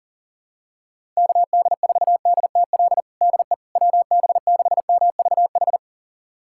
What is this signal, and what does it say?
Morse code, heard as a keyed tone, about 37 wpm, 700 Hz. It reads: KD4DTL DE WB6MVH
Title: USA Callsign DE Callsign - 6 Characters - 40wpm